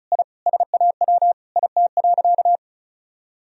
Morse code, heard as a keyed tone, about 34 wpm, 700 Hz. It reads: I SAW IT.
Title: Sentences from Top 1000 Words - Review Part 4 - 35wpm